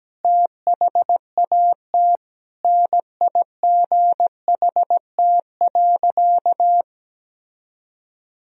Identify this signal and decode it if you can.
Morse code, heard as a keyed tone, about 17 wpm, 700 Hz. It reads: THAT NIGHT.
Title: Sentences from Top 1000 Words - Review Part 3 - 17wpm